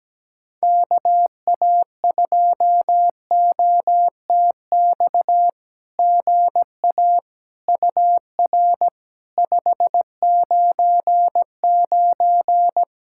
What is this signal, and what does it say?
Morse code, heard as a keyed tone, about 17 wpm, 700 Hz. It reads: KA2OTX GA UR 599
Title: SOTA QSO Practice 17wpm